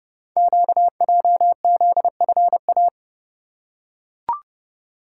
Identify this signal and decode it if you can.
Morse code, heard as a keyed tone, about 29 wpm, 690 Hz. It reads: QJZFA
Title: Code Groups - 5 Letters - 30wpm